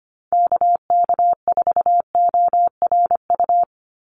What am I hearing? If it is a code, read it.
Morse code, heard as a keyed tone, about 25 wpm, 690 Hz. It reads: KK4ORU
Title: USA Call Signs 25wpm